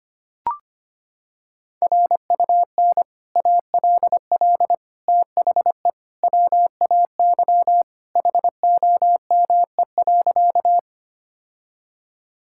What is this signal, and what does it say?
Morse code, heard as a keyed tone, about 25 wpm, 700 Hz. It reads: RUN ALL THE WAY HOME.
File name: practice / 25wpm / Sentences with 3-Word Phrases - 25wpm